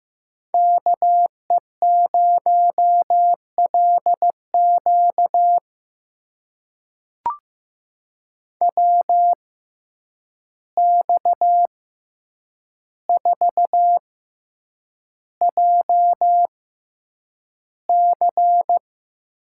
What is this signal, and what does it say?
Morse code, heard as a keyed tone, about 15 wpm, spaced as for 8 wpm, 700 Hz. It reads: KE0LQ WX4JC
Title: USA Call Signs - 6x Character Spacing - 15wpm